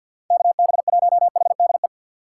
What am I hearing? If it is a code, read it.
Morse code, heard as a keyed tone, about 47 wpm, 700 Hz. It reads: KB1HDE
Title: USA Call Signs - 6 Characters - 8x Character Spacing - 50wpm